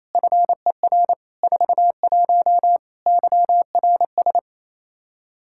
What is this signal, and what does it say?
Morse code, heard as a keyed tone, about 27 wpm, 700 Hz. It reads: FER 41 YRS
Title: K1USN SST QSO Practice 28wpm